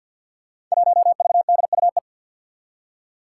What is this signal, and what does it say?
Morse code, heard as a keyed tone, about 49 wpm, 700 Hz. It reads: 1VDUE
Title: Five Challenging Letter-Number Combinations 50wpm